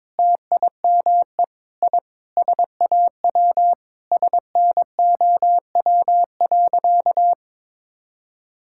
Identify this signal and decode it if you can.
Morse code, heard as a keyed tone, about 22 wpm, 700 Hz. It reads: TIME I SAW SNOW.
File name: practice / 22wpm / Sentences with 4-Word Phrases - 22wpm